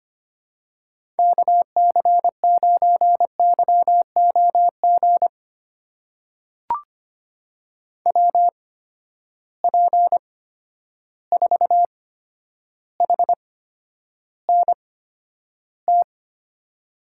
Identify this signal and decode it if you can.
Morse code, heard as a keyed tone, about 25 wpm, spaced as for 11 wpm, 700 Hz. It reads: KC9YOG WP4HNT
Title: USA Call Signs - 6 Characters - 8x Character Spacing - 25wpm